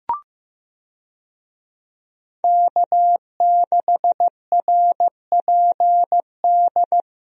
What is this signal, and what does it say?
Morse code, heard as a keyed tone, about 16 wpm, 700 Hz. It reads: K6RPD
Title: USA Call Signs - 5 Characters - Repeated 2x - 15wpm